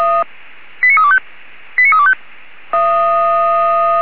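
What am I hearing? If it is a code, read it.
Five-tone selective call (ZVEI), heard as quick runs of five tones, tones of about 70 ms. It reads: 89325 89325
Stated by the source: ZVEI five-tone selective call received with an SDR, logged on the Feuerwehr Orlamünde audio server.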